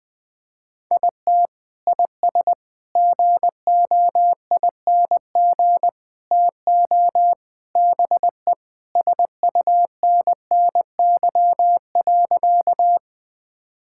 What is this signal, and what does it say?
Morse code, heard as a keyed tone, about 20 wpm, 700 Hz. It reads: IT IS GOING TO BE SUNNY.
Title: Sentences with 4-Word Phrases - 20wpm